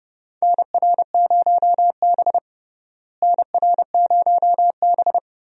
Morse code, heard as a keyed tone, about 30 wpm, 700 Hz. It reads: NR0B NR0B
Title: USA Call Signs - 4 Characters - Repeated 3x - 30wpm